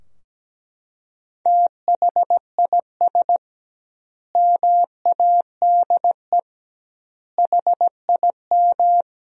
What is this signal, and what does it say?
Morse code, heard as a keyed tone, about 17 wpm, 700 Hz. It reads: THIS MADE HIM
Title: 2x Word-Spacing - Sentences from Top 500 Words - 17wpm